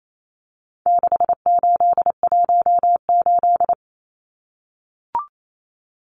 Morse code, heard as a keyed tone, about 28 wpm, 700 Hz. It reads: 6818
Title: Numbers - 4-Digit 28wpm